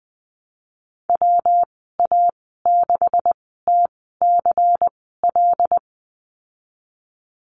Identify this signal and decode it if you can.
Morse code, heard as a keyed tone, about 20 wpm, spaced as for 15 wpm, 700 Hz. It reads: WA6TCL
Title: 179 Lesson 29g - Slash - Call Signs 20wpm